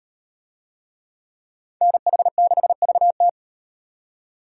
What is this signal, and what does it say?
Morse code, heard as a keyed tone, about 38 wpm, 700 Hz. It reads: NH6VT